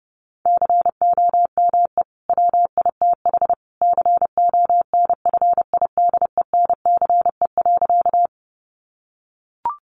Morse code, heard as a keyed tone, about 28 wpm, 700 Hz. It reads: COME WITH CONFIDENCE.
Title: Sentences from Top 1500 Words 30wpm